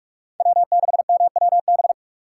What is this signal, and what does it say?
Morse code, heard as a keyed tone, about 46 wpm, 700 Hz. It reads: W6MWB
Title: USA Call Signs - 5 Characters - 2x Character Spacing - 45wpm